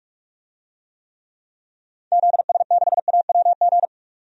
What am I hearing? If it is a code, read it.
Morse code, heard as a keyed tone, about 46 wpm, 700 Hz. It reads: ZS6AWG
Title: DXCC-100 Call Signs - 6 Characters - 45wpm